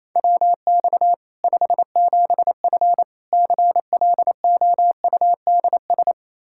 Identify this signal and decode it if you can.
Morse code, heard as a keyed tone, about 28 wpm, 700 Hz. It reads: WX 57F CLOUDS
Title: K1USN SST QSO Practice 28wpm